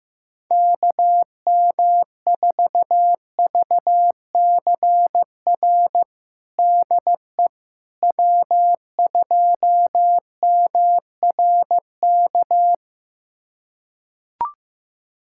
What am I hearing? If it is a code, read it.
Morse code, heard as a keyed tone, about 15 wpm, 700 Hz. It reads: KM4VCR DE W2MRK
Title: USA Callsign DE Callsign 15wpm